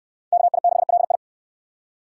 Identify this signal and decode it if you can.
Morse code, heard as a keyed tone, about 58 wpm, 700 Hz. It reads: KI6LI